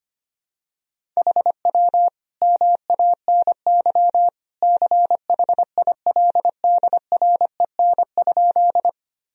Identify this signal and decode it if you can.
Morse code, heard as a keyed tone, about 25 wpm, 700 Hz. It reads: HW MANY CHILDREN?
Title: Sentences from Top 300 Words and Abbreviations 25wpm